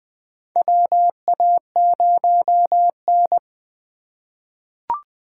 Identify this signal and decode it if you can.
Morse code, heard as a keyed tone, about 20 wpm, 700 Hz. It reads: WA0N
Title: USA Call Signs - 3x Character Spacing - 20wpm